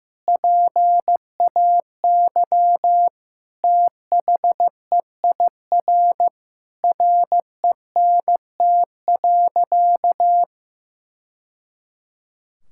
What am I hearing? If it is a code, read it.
Morse code, heard as a keyed tone, about 15 wpm, 700 Hz. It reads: PAY THEIR RENT.
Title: Sentences from Top 1000 Words - Review Part 5 - 15wpm